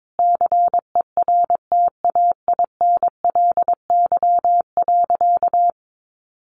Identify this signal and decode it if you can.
Morse code, heard as a keyed tone, about 22 wpm, 700 Hz. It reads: CERTAINLY.
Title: Sentences from Top 900 Words 22wpm